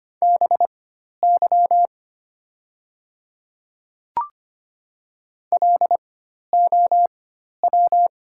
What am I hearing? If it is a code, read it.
Morse code, heard as a keyed tone, about 25 wpm, spaced as for 12 wpm, 700 Hz. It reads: BY LOW
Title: 4x Character Spacing - Top 200 Words - 25wpm